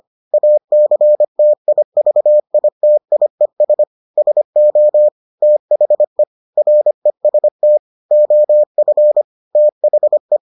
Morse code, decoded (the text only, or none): ACTIVITIES SO THE REST OF THE